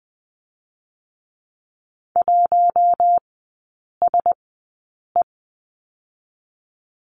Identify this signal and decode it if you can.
Morse code, heard as a keyed tone, about 20 wpm, spaced as for 8 wpm, 700 Hz. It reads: 1SE
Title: 121 Lesson 20e - Number 3 - Three Characters 20wpm